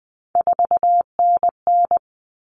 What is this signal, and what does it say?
Morse code, heard as a keyed tone, about 20 wpm, 700 Hz. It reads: 4NN
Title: All Cut Numbers - 3-Digit 20wpm